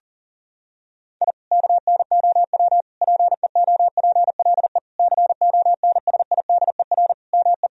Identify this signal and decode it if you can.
Morse code, heard as a keyed tone, about 37 wpm, 700 Hz. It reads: I KNOW PEOPLE CONSIDER ME
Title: Sentences from Top 1000 Words - Review Part 4 - 40wpm